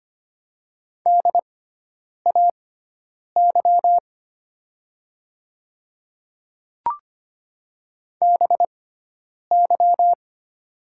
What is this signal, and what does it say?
Morse code, heard as a keyed tone, about 25 wpm, spaced as for 9 wpm, 700 Hz. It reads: DAY BY
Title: 6x Character Spacing - Top 100 Words - 25wpm